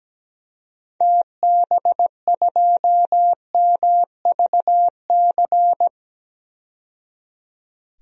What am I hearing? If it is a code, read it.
Morse code, heard as a keyed tone, about 17 wpm, 700 Hz. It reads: TB2MVC
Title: DXCC-100 Call Signs - 6 Characters - 17wpm